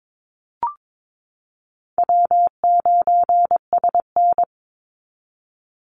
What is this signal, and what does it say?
Morse code, heard as a keyed tone, about 22 wpm, 700 Hz. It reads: W9SN
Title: Call Signs Repeated 3x - 22wpm